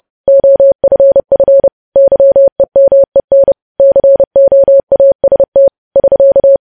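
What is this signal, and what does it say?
Morse code, heard as a keyed tone, about 30 wpm, 550 Hz. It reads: OFF YEMEN COAST <SK>
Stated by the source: Morse code practice of news headlines at 30.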